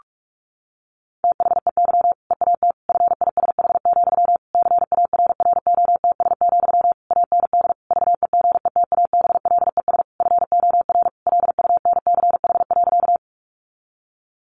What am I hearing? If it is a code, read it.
Morse code, heard as a keyed tone, about 44 wpm, 700 Hz. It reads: THEY EAT FISH, CARROTS, AND VEGETABLES FOR LUNCH.
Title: Easy Phrases 45wpm